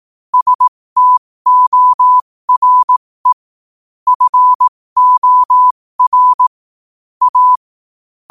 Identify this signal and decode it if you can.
Morse code, heard as a keyed tone, about 18 wpm, spaced as for 15 wpm, 1000 Hz. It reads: STORE FOR A
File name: qotdfiles / 15wpm